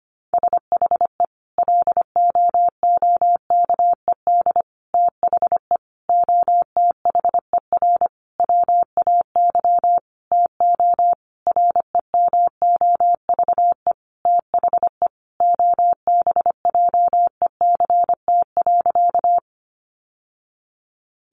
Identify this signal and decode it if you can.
Morse code, heard as a keyed tone, about 25 wpm, 700 Hz. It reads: SHE LOOKED THE OTHER WAY TO REMOVE THE OBJECT.
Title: Sentences from Top 800 Words 25wpm